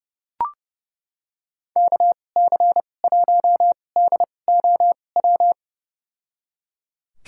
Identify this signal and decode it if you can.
Morse code, heard as a keyed tone, about 30 wpm, spaced as for 22 wpm, 700 Hz. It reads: KC1DOW
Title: USA Call Signs - 2x Character Spacing - 30wpm